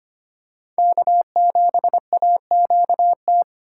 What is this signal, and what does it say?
Morse code, heard as a keyed tone, about 25 wpm, 700 Hz. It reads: K7AQT
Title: USA Call Signs - 5 Characters - 4x Character Spacing - 25wpm